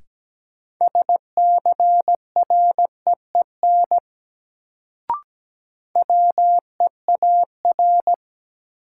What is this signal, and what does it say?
Morse code, heard as a keyed tone, about 17 wpm, 700 Hz. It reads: SCREEN WEAR
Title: Sentences from Top 1000 Words - Review Part 6 - New Word Drill - 17wpm